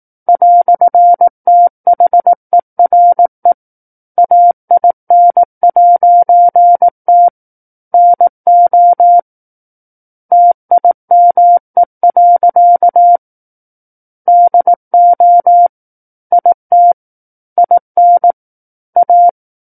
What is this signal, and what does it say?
Morse code, heard as a keyed tone, about 18 wpm, 700 Hz. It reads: "THERE AIN'T NO TIME. DO IT IN A
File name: 18 WPM